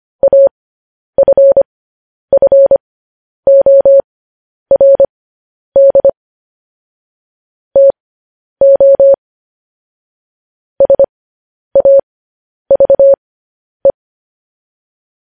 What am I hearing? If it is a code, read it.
Morse code, heard as a keyed tone, about 25 wpm, spaced as for 10 wpm, 550 Hz. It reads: AFFORD TO SAVE
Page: news headlines in morse code at 10 wpm